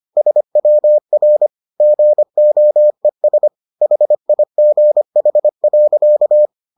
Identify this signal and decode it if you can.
Morse code, heard as a keyed tone, about 25 wpm, 600 Hz. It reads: SWR GOES HIGH.